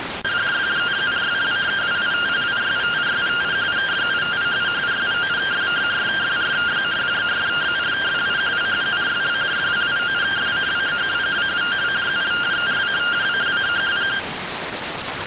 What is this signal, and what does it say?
Two-tone teleprinter signal, about 170 Hz shift.